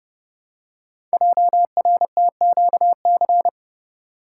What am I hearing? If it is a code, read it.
Morse code, heard as a keyed tone, about 30 wpm, 690 Hz. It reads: JRTQC
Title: Code Groups - 5 Letters - 30wpm